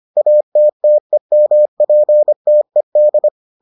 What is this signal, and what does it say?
Morse code, heard as a keyed tone, about 25 wpm, 600 Hz. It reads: ATTEMPTED